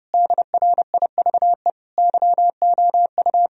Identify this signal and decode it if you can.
Morse code, heard as a keyed tone, about 29 wpm, 700 Hz. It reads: DRIVE YOU
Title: Sentences from Top 1000 Words - Review Part 2 - 30wpm